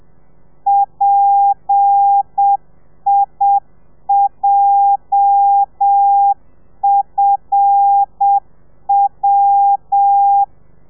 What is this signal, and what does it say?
Morse code, heard as a keyed tone, about 7 wpm, 800 Hz. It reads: PIJFW
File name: telegrafia / i2tfj / Lez8 30